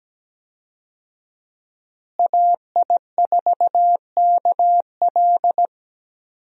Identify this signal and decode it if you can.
Morse code, heard as a keyed tone, about 17 wpm, 700 Hz. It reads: AI4KL
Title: USA Call Signs - 5 Characters - Repeated 2x - 17wpm